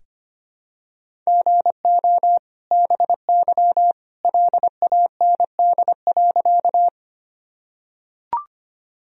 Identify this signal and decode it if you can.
Morse code, heard as a keyed tone, about 25 wpm, 700 Hz. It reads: GO BY LAND.
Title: Sentences from Top 1000 Words - Review Part 1 - 25wpm